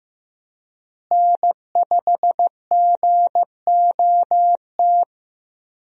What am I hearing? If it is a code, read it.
Morse code, heard as a keyed tone, about 15 wpm, 700 Hz. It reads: N5GOT
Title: USA Call Signs - 5 Characters - Repeated 2x - 15wpm